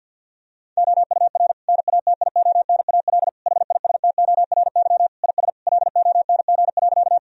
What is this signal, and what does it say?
Morse code, heard as a keyed tone, about 50 wpm, 700 Hz. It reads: OUR NATIONAL HISTORY IS LONG.
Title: Sentences from Top 1000 Words - Review Part 4 - 50wpm